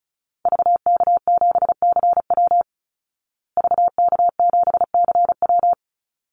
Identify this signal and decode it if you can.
Morse code, heard as a keyed tone, about 34 wpm, 700 Hz. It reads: VK7CW VK7CW
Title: Call Signs Repeated 2x - 35wpm